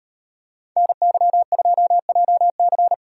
Morse code, heard as a keyed tone, about 31 wpm, 700 Hz. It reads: NY2JC
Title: Speed Racing - USA Call Signs 25wpm